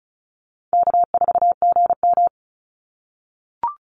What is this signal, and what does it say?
Morse code, heard as a keyed tone, about 34 wpm, 700 Hz. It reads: K4GM
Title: Call Signs - 35wpm